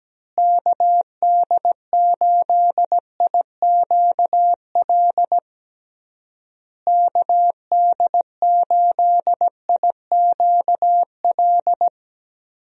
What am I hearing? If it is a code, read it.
Morse code, heard as a keyed tone, about 17 wpm, 700 Hz. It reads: KD8IQL KD8IQL
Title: USA Call Signs - 6 Characters - Repeated 3x - 17wpm